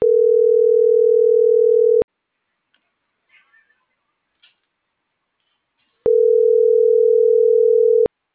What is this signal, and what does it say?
Ringback tone (North America), 440 + 480 Hz.